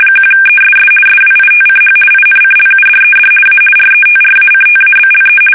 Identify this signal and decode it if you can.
RTTY, about 50 baud. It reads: 1ESS SWITCH INCLUDED A RETRACTABLE
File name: fsk850-50B / teletype-corp